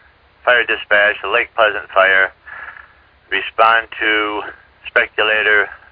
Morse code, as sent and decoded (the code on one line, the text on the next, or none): none